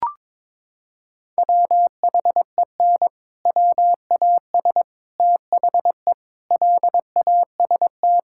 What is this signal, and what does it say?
Morse code, heard as a keyed tone, about 22 wpm, 700 Hz. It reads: WHEN WAS THE LAST
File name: practice / 22wpm / Sentences with 7-Word Phrases - 22wpm